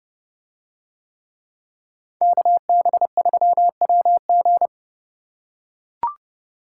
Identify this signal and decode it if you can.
Morse code, heard as a keyed tone, about 29 wpm, 700 Hz. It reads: KB3WG